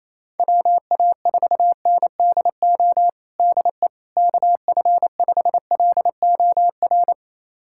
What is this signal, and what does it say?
Morse code, heard as a keyed tone, about 28 wpm, 700 Hz. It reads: WA4NDO DE KF5LOR
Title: USA Callsign DE Callsign 28wpm